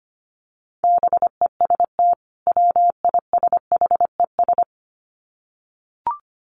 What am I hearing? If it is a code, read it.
Morse code, heard as a keyed tone, about 25 wpm, 700 Hz. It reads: BEST WISHES
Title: Common QSO Phrases 25wpm